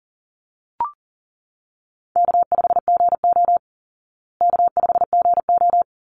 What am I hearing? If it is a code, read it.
Morse code, heard as a keyed tone, about 40 wpm, 700 Hz. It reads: K5GO K5GO
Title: Call Signs Repeated 2x - 40wpm